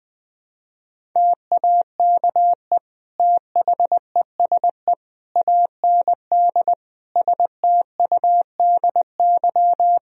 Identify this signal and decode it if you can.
Morse code, heard as a keyed tone, about 20 wpm, 700 Hz. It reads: TAKE THESE AND STUDY
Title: Sentences from Top 1000 Words - Review Part 1 - 20wpm